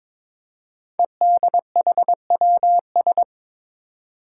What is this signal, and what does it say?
Morse code, heard as a keyed tone, about 22 wpm, 700 Hz. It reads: EDHWS
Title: Five Challenging Letter-Number Combinations 22wpm